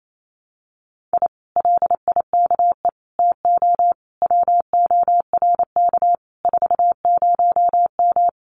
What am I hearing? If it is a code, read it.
Morse code, heard as a keyed tone, about 28 wpm, 700 Hz. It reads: I LIKE TO WORK 40M
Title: Common QSO Phrases Encore 28wpm